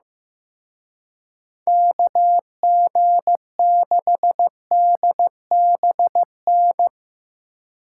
Morse code, heard as a keyed tone, about 16 wpm, 700 Hz. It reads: KG6DBN
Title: Speed Racing - USA Call Signs 15wpm